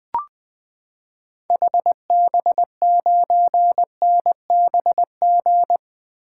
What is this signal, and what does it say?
Morse code, heard as a keyed tone, about 20 wpm, 700 Hz. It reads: HB9NBG
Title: 2023 SOTA Activator Call Signs - 20wpm